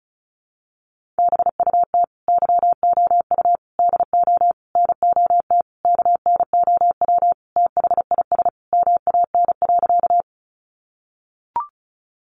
Morse code, heard as a keyed tone, about 34 wpm, 700 Hz. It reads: BUT YOU DO NOT KNOW THIS MAN.
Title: Sentences from Top 600 Words 35wpm